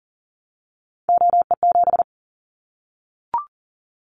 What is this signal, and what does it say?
Morse code, heard as a keyed tone, about 40 wpm, 700 Hz. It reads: OE7
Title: Callsign Prefixes for Top 25 DXCC - Repeated 3x - 40wpm